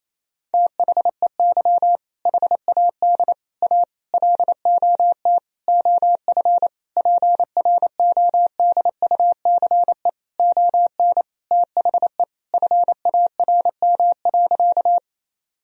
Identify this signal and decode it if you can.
Morse code, heard as a keyed tone, about 28 wpm, 700 Hz. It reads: THEY HAD A LOT OF PRODUCE ON THE FARM.
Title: Sentences from Top 1000 Words - Review Part 3 - 28wpm